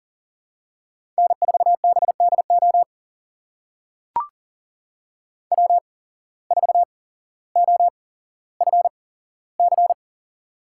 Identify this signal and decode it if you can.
Morse code, heard as a keyed tone, about 39 wpm, spaced as for 17 wpm, 700 Hz. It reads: N4BDO W4OFC
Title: USA Call Signs - 5 Characters - 8x Character Spacing - 40wpm